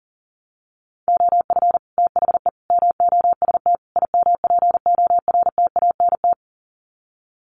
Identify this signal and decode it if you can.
Morse code, heard as a keyed tone, about 37 wpm, 700 Hz. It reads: OF THE MOST IMPORTANT
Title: Sets of 4 Words 40wpm